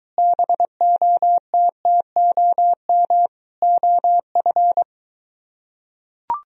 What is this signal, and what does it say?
Morse code, heard as a keyed tone, about 24 wpm, 700 Hz. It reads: BOTTOM OF